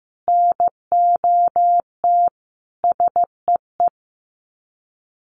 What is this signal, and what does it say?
Morse code, heard as a keyed tone, about 15 wpm, 700 Hz. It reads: NOT SEE
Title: Sets of 5 Words Encore 15wpm